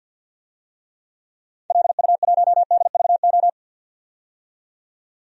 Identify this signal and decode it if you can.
Morse code, heard as a keyed tone, about 48 wpm, 700 Hz. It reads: LU1DVO
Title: DXCC-100 Call Signs - 6 Characters - 50wpm